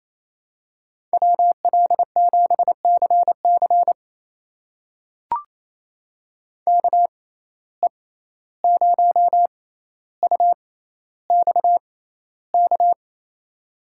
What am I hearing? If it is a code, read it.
Morse code, heard as a keyed tone, about 28 wpm, spaced as for 14 wpm, 700 Hz. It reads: WL7CC KE0UXK
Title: USA Call Signs - 6x Character Spacing - 28wpm